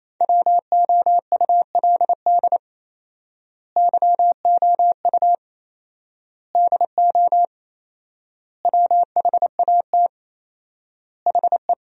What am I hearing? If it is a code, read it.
Morse code, heard as a keyed tone, about 28 wpm, 700 Hz. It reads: WOULD YOU DO WHAT HE
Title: 4x Word-Spacing - Sentences from Top 100 Words - 28wpm